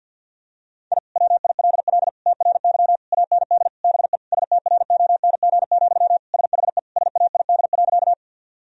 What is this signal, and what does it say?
Morse code, heard as a keyed tone, about 52 wpm, 700 Hz. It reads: I WILL TRY AND BE STRONG, SHE SAID.
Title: Sentences from Top 1000 Words - Review Part 3 - 50wpm